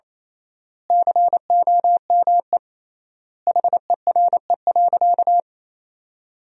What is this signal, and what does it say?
Morse code, heard as a keyed tone, about 28 wpm, 700 Hz. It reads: COME HERE.
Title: 3x Word-Spacing - Sentences from Top 200 Words - 28wpm